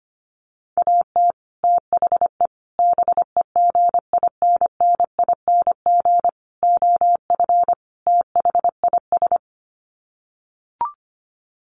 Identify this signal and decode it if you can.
Morse code, heard as a keyed tone, about 25 wpm, 700 Hz. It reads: AT THE BEGINNING OF THIS
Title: Sets of 5 Words Encore 25wpm